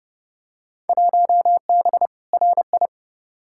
Morse code, heard as a keyed tone, about 32 wpm, 700 Hz. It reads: 1B RI
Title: Field Day Exchanges Encore - 30wpm